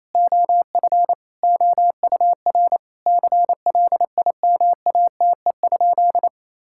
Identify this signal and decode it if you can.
Morse code, heard as a keyed tone, about 28 wpm, 700 Hz. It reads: OF OUR CLIMATE?
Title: Sentences from Top 1600 Words 28wpm